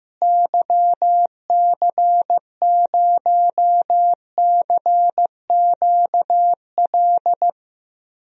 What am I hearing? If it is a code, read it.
Morse code, heard as a keyed tone, about 15 wpm, 700 Hz. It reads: YC0CQL
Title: DXCC-100 Call Signs - 6 Characters - Repeated 3x - 15wpm